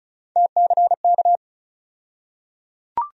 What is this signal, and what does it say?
Morse code, heard as a keyed tone, about 36 wpm, 700 Hz. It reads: TCK
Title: USA Call Sign Postfixes - 3 Letters 35wpm